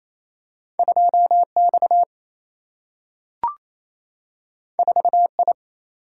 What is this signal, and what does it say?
Morse code, heard as a keyed tone, about 28 wpm, 700 Hz. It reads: 2X 4I
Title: Single-Letter Single-Number 28wpm